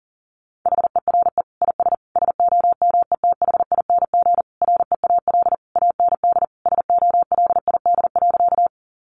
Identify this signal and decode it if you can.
Morse code, heard as a keyed tone, about 40 wpm, 700 Hz. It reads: HERE IS SOMETHING REAL AND SOLID.
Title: Sentences from Top 1500 Words 40wpm